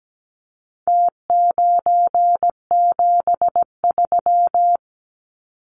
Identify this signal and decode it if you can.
Morse code, heard as a keyed tone, about 17 wpm, 700 Hz. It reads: T973
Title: Numbers - Contest Style - 4-Digit 17wpm